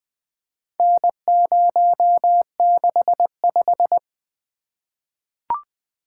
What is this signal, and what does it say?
Morse code, heard as a keyed tone, about 20 wpm, 700 Hz. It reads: N065
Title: Four Letter-Number Combinations - 20wpm